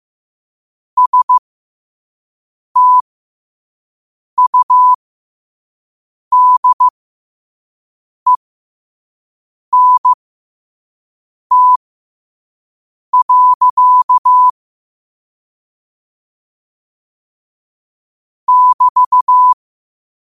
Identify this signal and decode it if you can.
Morse code, heard as a keyed tone, about 15 wpm, spaced as for 5 wpm, 1000 Hz. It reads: STUDENT. <BT>
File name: qotdfiles / 5wpm